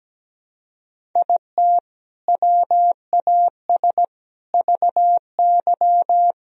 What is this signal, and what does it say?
Morse code, heard as a keyed tone, about 17 wpm, 700 Hz. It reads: IT WAS VY
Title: Sentences from Top 300 Words and Abbreviations 17wpm